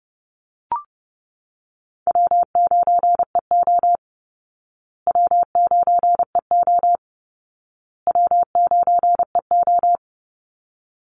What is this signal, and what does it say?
Morse code, heard as a keyed tone, about 30 wpm, 700 Hz. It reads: W9EO W9EO W9EO
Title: Call Signs Repeated 2x - 30wpm